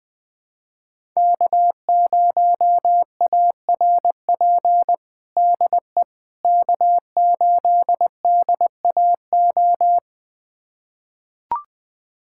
Morse code, heard as a keyed tone, about 20 wpm, 700 Hz. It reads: K0ARP DE K8DAO
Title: USA Callsign DE Callsign - 5 Characters - 20wpm